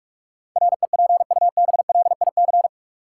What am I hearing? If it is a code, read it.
Morse code, heard as a keyed tone, about 41 wpm, 700 Hz. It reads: REPUBLIC
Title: Sentences from Top 1500 Words - New Word Drill - 45wpm